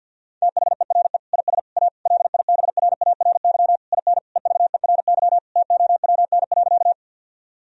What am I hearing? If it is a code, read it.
Morse code, heard as a keyed tone, about 50 wpm, 700 Hz. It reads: THERE IS A LIBRARY IN EVERY TOWN.
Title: Sentences from Top 1000 Words - Review Part 4 - 50wpm